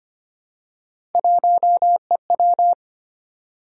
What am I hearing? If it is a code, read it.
Morse code, heard as a keyed tone, about 25 wpm, 700 Hz. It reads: 1EW
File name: Three Letter-Number Combinations - 25wpm